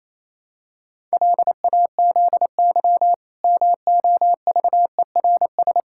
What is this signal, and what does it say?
Morse code, heard as a keyed tone, about 28 wpm, 700 Hz. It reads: LAZY MOVERS
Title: Pangram Challenge v2 - 28wpm